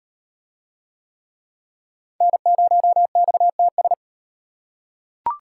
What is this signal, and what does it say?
Morse code, heard as a keyed tone, about 38 wpm, 700 Hz. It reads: N0XTS